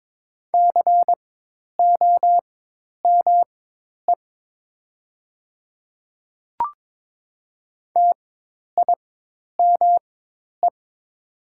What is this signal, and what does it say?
Morse code, heard as a keyed tone, about 22 wpm, spaced as for 10 wpm, 700 Hz. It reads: COME TIME
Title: 4x Character Spacing - Top 100 Words - 22wpm